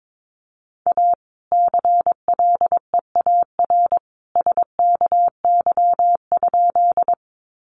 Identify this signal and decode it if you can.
Morse code, heard as a keyed tone, about 22 wpm, 700 Hz. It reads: A CLEAR SKY?
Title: Sentences from Top 1200 Words 22wpm